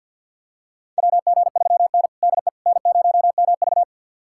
Morse code, heard as a keyed tone, about 53 wpm, 700 Hz. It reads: WG3N DE N0MV
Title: USA Callsign DE Callsign - 4 Characters - 50wpm